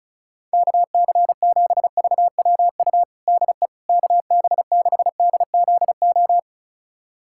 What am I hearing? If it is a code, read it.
Morse code, heard as a keyed tone, about 36 wpm, 700 Hz. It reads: KC7VWU DE KB6DZO
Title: USA Callsign DE Callsign 35wpm